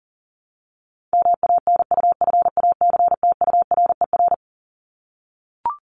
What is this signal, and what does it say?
Morse code, heard as a keyed tone, about 38 wpm, 700 Hz. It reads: MANUFACTURER
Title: Top 1751-2000 Words 40wpm